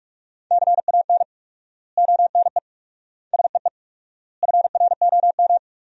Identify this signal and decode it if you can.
Morse code, heard as a keyed tone, about 46 wpm, 700 Hz. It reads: CAN ONE SEE FROM
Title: 4x Word-Spacing - Sentences from Top 100 Words - 45wpm